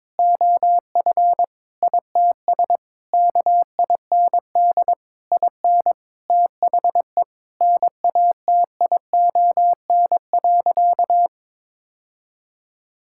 Morse code, heard as a keyed tone, about 22 wpm, 700 Hz. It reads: OF ITS KIND IN THE NATION.
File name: practice / 22wpm / Sentences with 8-Word Phrases - 22wpm